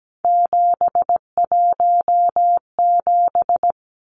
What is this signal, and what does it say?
Morse code, heard as a keyed tone, about 17 wpm, 700 Hz. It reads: 717
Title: Numbers - 3-Digit 17wpm